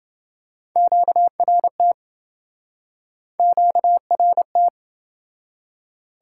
Morse code, heard as a keyed tone, about 28 wpm, 700 Hz. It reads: QRT QRT